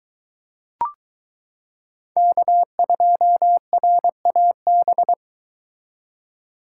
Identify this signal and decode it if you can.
Morse code, heard as a keyed tone, about 22 wpm, 700 Hz. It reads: K2RAB